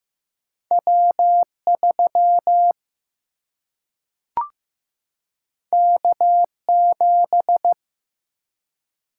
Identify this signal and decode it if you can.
Morse code, heard as a keyed tone, about 15 wpm, 700 Hz. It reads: W3 K7
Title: Single-Letter Single-Number 15wpm